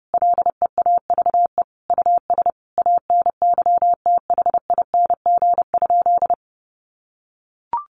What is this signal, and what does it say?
Morse code, heard as a keyed tone, about 28 wpm, 700 Hz. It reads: LEAVE US ANYTHING?
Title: Sentences from Top 1100 Words 30wpm